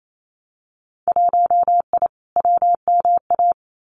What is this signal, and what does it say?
Morse code, heard as a keyed tone, about 28 wpm, 700 Hz. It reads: 1I WMA
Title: Winter Field Day Exchanges Encore 28wpm